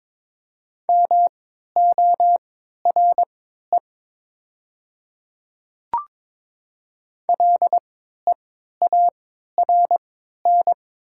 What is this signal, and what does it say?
Morse code, heard as a keyed tone, about 22 wpm, spaced as for 12 wpm, 700 Hz. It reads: MORE LEARN
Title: 3x Character Spacing - Top 300 Words - 22wpm